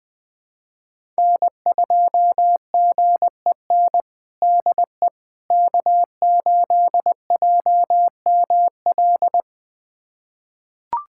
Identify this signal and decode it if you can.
Morse code, heard as a keyed tone, about 20 wpm, 700 Hz. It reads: N2GEN DE K8JML
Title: USA Callsign DE Callsign - 5 Characters - 20wpm